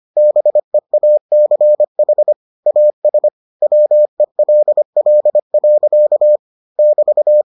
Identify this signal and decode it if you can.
Morse code, heard as a keyed tone, about 25 wpm, 600 Hz. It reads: BEACH AS WELL. <BT>